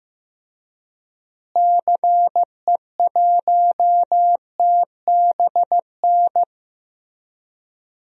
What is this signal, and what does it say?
Morse code, heard as a keyed tone, about 16 wpm, 700 Hz. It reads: CE1TBN
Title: DXCC-100 Call Signs - 6 Characters - Repeated 3x - 15wpm